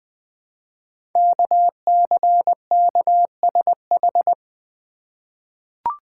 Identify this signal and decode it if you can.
Morse code, heard as a keyed tone, about 21 wpm, 700 Hz. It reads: KCKSH
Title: Code Groups - 5 Letters - 20wpm